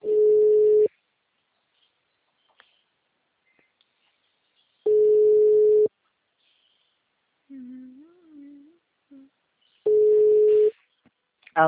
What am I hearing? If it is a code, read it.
Ringback tone (Europe, 425 Hz), with speech.
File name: out- -2029- - - .0